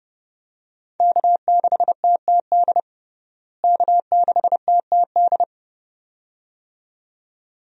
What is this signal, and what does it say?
Morse code, heard as a keyed tone, about 32 wpm, 700 Hz. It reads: K6TTD K6TTD
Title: USA Call Signs - 5 Characters - Repeated 2x - 30wpm